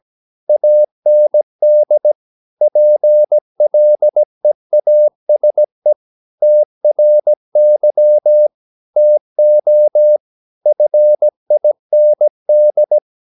Morse code, heard as a keyed tone, about 17 wpm, 600 Hz. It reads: AND PLEASE TRY TO FIND
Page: Romanen Never af Ken Follet på engelsk.